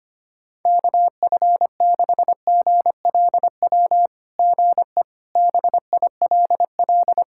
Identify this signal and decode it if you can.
Morse code, heard as a keyed tone, about 25 wpm, 700 Hz. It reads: KF6GLW GE BILL
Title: SOTA QSO Practice 25wpm